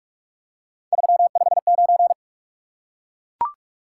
Morse code, heard as a keyed tone, about 44 wpm, 700 Hz. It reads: 359
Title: Top 180 QSO Elements 45wpm